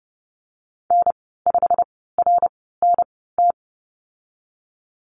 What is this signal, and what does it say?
Morse code, heard as a keyed tone, about 30 wpm, spaced as for 17 wpm, 700 Hz. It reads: N5RNT
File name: Lesson 19g - Letter w - Call Signs 30wpm